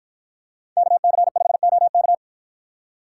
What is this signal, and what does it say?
Morse code, heard as a keyed tone, about 48 wpm, 700 Hz. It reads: KC5OK